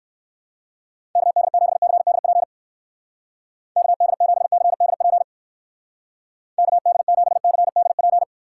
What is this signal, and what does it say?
Morse code, heard as a keyed tone, about 60 wpm, 700 Hz. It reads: KD7CDP KD7CDP KD7CDP